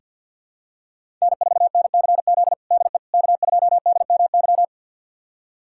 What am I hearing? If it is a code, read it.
Morse code, heard as a keyed tone, about 48 wpm, 700 Hz. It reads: N4NCZ DE K2DMY
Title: USA Callsign DE Callsign - 5 Characters - 50wpm